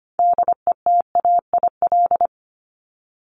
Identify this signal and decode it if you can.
Morse code, heard as a keyed tone, about 25 wpm, 700 Hz. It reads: DETAIL
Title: Top 1251-1500 Words 25wpm